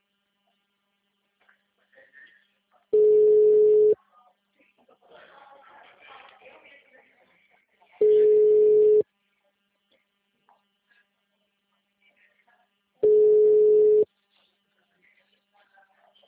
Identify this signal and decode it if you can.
Ringback tone (Europe, 425 Hz), with speech.
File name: out- -2017- - - .71536